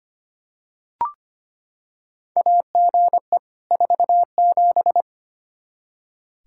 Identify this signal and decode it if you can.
Morse code, heard as a keyed tone, about 25 wpm, 700 Hz. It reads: AGE 47
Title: K1USN SST QSO Practice 25wpm